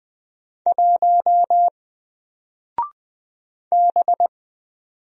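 Morse code, heard as a keyed tone, about 20 wpm, 700 Hz. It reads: 1 B
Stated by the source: Single letter and number Morse drill at 20 wpm, from N4LCD.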